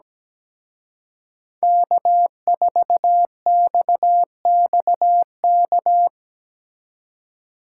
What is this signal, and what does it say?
Morse code, heard as a keyed tone, about 19 wpm, 700 Hz. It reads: K4XXK
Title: Speed Racing - USA 1x3 Call Signs 17wpm